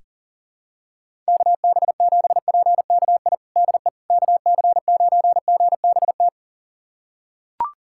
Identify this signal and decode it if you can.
Morse code, heard as a keyed tone, about 37 wpm, 700 Hz. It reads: KB7PKI DE KC9GBT
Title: USA Callsign DE Callsign - 6 Characters - 40wpm